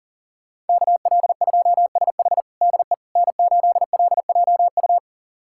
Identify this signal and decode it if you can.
Morse code, heard as a keyed tone, about 40 wpm, 700 Hz. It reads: KL2SH DE N8LJU
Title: USA Callsign DE Callsign 40wpm